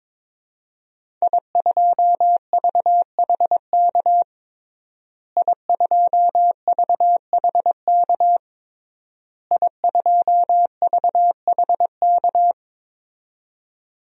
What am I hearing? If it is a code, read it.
Morse code, heard as a keyed tone, about 22 wpm, 700 Hz. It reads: I2VHK I2VHK I2VHK
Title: DXCC-100 Call Signs - 5 Characters - Repeated 3x - 22wpm